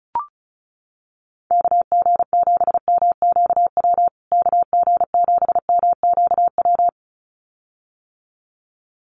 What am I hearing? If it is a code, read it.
Morse code, heard as a keyed tone, about 34 wpm, 700 Hz. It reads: KG7MQW KG7MQW
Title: USA Call Signs Repeated 2x 35wpm